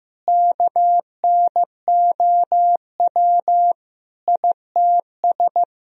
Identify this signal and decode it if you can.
Morse code, heard as a keyed tone, about 16 wpm, 700 Hz. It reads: KNOW ITS
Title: Sentences from Top 1000 Words - Review Part 2 - 15wpm